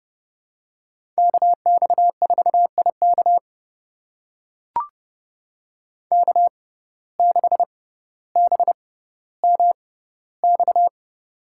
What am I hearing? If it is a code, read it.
Morse code, heard as a keyed tone, about 30 wpm, spaced as for 15 wpm, 700 Hz. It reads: KX4IK K6BMX
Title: USA Call Signs - 5 Characters - 6x Character Spacing - 30wpm